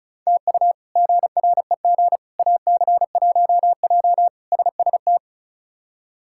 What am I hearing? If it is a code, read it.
Morse code, heard as a keyed tone, about 34 wpm, 700 Hz. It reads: TU GREG AC1J SST
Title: K1USN SST QSO Practice - 35wpm